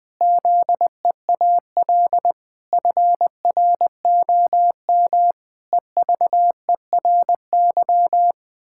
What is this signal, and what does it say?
Morse code, heard as a keyed tone, about 21 wpm, 700 Hz. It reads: ZEAL FROM EVERY
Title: Pangram Challenge v2 - 20wpm